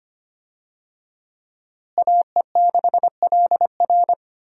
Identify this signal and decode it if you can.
Morse code, heard as a keyed tone, about 25 wpm, 700 Hz. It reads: AE6LR
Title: USA Call Signs - 5 Characters - Repeated 2x - 25wpm